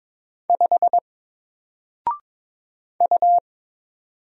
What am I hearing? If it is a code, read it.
Morse code, heard as a keyed tone, about 22 wpm, 700 Hz. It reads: 5 U
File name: Single-Letter-Number-22wpm